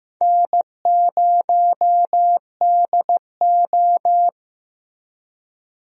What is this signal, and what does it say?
Morse code, heard as a keyed tone, about 15 wpm, 700 Hz. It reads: N0DO
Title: USA Call Signs - 4 Characters - Repeated 2x - 15wpm